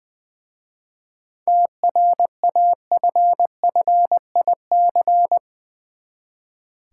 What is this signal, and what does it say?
Morse code, heard as a keyed tone, about 20 wpm, 700 Hz. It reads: TRAFFIC
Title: Easy Words with Hints - 20wpm